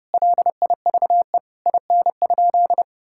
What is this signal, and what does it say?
Morse code, heard as a keyed tone, about 30 wpm, 700 Hz. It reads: LIVE IN?
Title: Sentences from Top 1000 Words - Review Part 5 - 30wpm